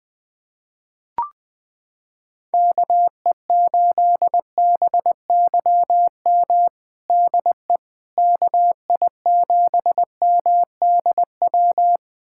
Morse code, heard as a keyed tone, about 20 wpm, 700 Hz. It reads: KE8BYM DE KI7MDW
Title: USA Callsign DE Callsign - 6 Characters - 20wpm